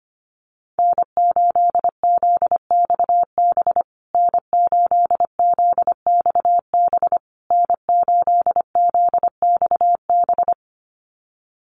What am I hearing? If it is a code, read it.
Morse code, heard as a keyed tone, about 25 wpm, 700 Hz. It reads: N8ZXB N8ZXB N8ZXB
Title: USA Call Signs Repeated 3x 25wpm